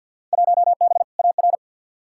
Morse code, heard as a keyed tone, about 46 wpm, 700 Hz. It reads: 1B AR
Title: Field Day Exchanges Encore - 50wpm